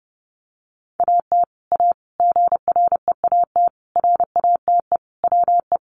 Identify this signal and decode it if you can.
Morse code, heard as a keyed tone, about 30 wpm, 700 Hz. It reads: AT A GREAT RATE WE
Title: Sentences from Top 1200 Words 30wpm